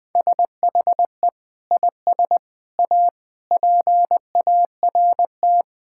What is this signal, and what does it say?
Morse code, heard as a keyed tone, about 20 wpm, 700 Hz. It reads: SHE IS A PART
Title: Sentences from Top 1000 Words - Review Part 5 - 20wpm